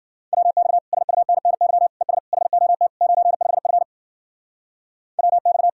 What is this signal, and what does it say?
Morse code, heard as a keyed tone, about 48 wpm, 700 Hz. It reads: WX SUNNY ES HOT 95F WX
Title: Speed-Racing Common QSO Phrases - 40wpm